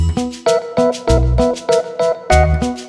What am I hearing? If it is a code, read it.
There is no code.